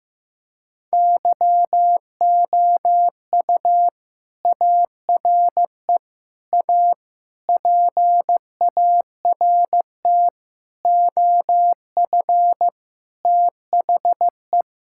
Morse code, heard as a keyed tone, about 16 wpm, 700 Hz. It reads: YOU ARE A PART OF THE
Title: Sentences from Top 1000 Words - Review Part 6 - 15wpm